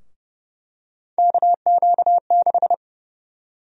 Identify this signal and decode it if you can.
Morse code, heard as a keyed tone, about 29 wpm, 700 Hz. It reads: KQ6
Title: USA Call Sign Prefixes - Repeated 2x - 30wpm